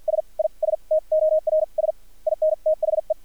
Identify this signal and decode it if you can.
Morse code, heard as a keyed tone, about 50 wpm, spaced as for 39 wpm, 630 Hz. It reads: SISTORS IN THE
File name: archived / CRSnet